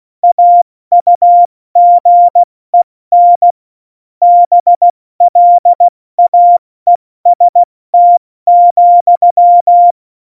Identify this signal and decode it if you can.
Morse code, heard as a keyed tone, about 16 wpm, 700 Hz. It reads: AUGEN BLAEST,
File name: Morsekurs / Witze / Witz 06. November 2018, 70 F80